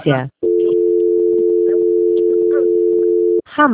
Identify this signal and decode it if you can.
Dial tone (North America), 350 + 440 Hz, with speech.
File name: out- - - - - .19365